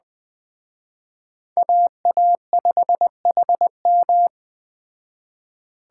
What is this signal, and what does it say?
Morse code, heard as a keyed tone, about 21 wpm, 700 Hz. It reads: AA5HM
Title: Speed Racing - USA 2x2 Call Signs 17wpm